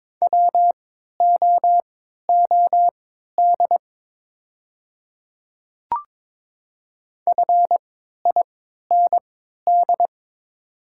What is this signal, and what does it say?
Morse code, heard as a keyed tone, about 23 wpm, spaced as for 13 wpm, 700 Hz. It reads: WOOD FIND
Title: 3x Character Spacing - Top 300 Words - 22wpm